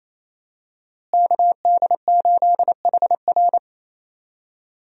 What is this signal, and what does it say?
Morse code, heard as a keyed tone, about 28 wpm, 700 Hz. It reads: KD8HR
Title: USA Call Signs - 5 Characters - Repeated 3x - 28wpm